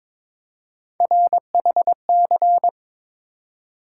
Morse code, heard as a keyed tone, about 22 wpm, 700 Hz. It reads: RHC
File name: practice / 22wpm / Three Letter-Number Combinations - 22wpm